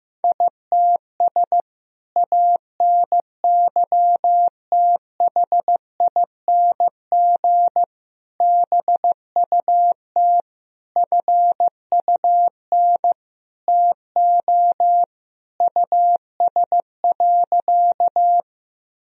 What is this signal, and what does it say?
Morse code, heard as a keyed tone, about 15 wpm, 700 Hz. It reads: ITS ANYTHING BUT FUN TO US.
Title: Sentences from Top 1000 Words - Review Part 4 - 15wpm